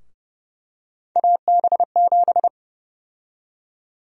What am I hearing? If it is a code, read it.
Morse code, heard as a keyed tone, about 30 wpm, 700 Hz. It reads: AB7
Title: USA Call Sign Prefixes - Repeated 2x - 30wpm